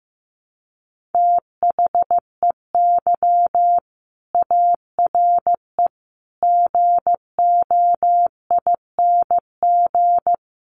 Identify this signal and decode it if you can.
Morse code, heard as a keyed tone, about 15 wpm, 700 Hz. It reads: THEY ARE GOING
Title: Sets of 4 Words 15wpm